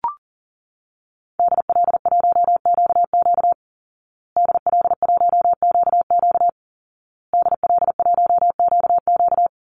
Morse code, heard as a keyed tone, about 40 wpm, 700 Hz. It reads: DL1QQ DL1QQ DL1QQ
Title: Call Signs Repeated 2x - 40wpm